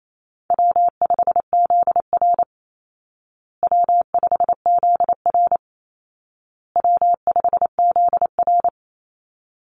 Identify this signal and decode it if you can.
Morse code, heard as a keyed tone, about 28 wpm, 700 Hz. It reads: W5ZR W5ZR W5ZR
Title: Easy Call Signs Repeated 3x - 28wpm